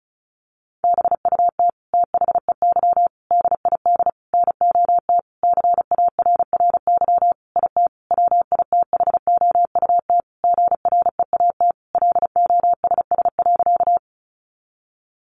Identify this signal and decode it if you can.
Morse code, heard as a keyed tone, about 34 wpm, 700 Hz. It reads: BUT THEY DID NOT CARRY IT WITHOUT GREAT LOSS.
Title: Sentences from Top 1400 Words 35wpm